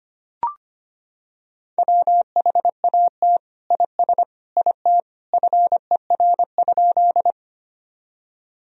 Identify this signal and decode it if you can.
Morse code, heard as a keyed tone, about 25 wpm, 700 Hz. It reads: WHAT IS IT FER?
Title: Sentences from Top 100 Words and Abbreviations 25wpm